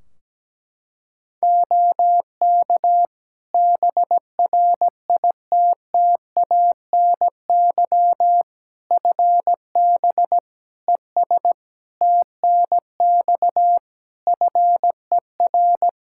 Morse code, heard as a keyed tone, about 17 wpm, 700 Hz. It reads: OK BRITTANY FB ES TNX FER
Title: Practice QSO - Exchange 2 - Info - 17wpm